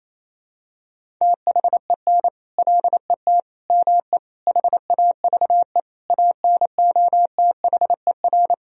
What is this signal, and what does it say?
Morse code, heard as a keyed tone, about 27 wpm, 700 Hz. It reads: THEN LET ME HAVE ANOTHER
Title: Sentences from Top 1000 Words - Review Part 2 - 28wpm